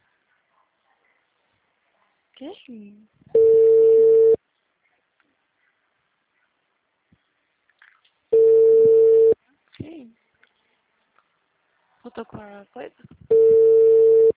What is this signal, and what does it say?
Ringback tone (China, 450 Hz), with speech.
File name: out- -2003- - - .220